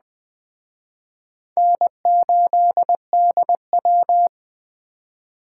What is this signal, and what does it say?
Morse code, heard as a keyed tone, about 21 wpm, 700 Hz. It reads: N8DW
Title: Speed Racing - USA Call Signs 17wpm